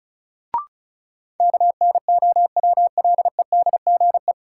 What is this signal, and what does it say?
Morse code, heard as a keyed tone, about 36 wpm, 700 Hz. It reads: KNOWLEDGE
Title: Sentences from Top 600 Words - New Word Drill - 35wpm